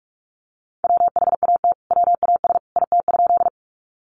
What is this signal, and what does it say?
Morse code, heard as a keyed tone, about 44 wpm, 700 Hz. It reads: WHAT WAS IT?
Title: Sentences from Top 100 Words 45wpm